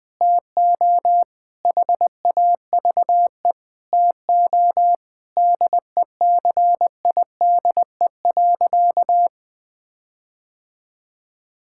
Morse code, heard as a keyed tone, about 20 wpm, 700 Hz. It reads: TO HAVE TO DECIDE.
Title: Sentences with 4-Word Phrases - 20wpm